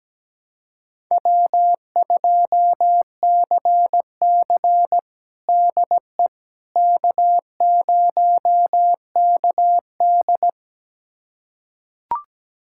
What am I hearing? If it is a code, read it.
Morse code, heard as a keyed tone, about 17 wpm, 700 Hz. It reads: W2CC DE K0KD
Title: USA Callsign DE Callsign - 4 Characters - 17wpm